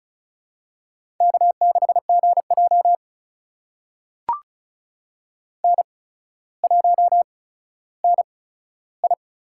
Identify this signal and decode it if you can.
Morse code, heard as a keyed tone, about 34 wpm, spaced as for 15 wpm, 700 Hz. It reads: K6GJ N1NI
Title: USA Call Signs - 4 Characters - 8x Character Spacing - 35wpm